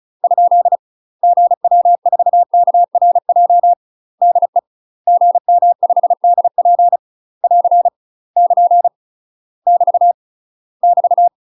Morse code, heard as a keyed tone, about 34 wpm, 700 Hz. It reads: ? GW4KRJ DE GM5DP <AR> <KN> <BT> <BT>
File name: UK MASTER QSOs TRAINING / 35WPM / UK MASTER QSOs 0000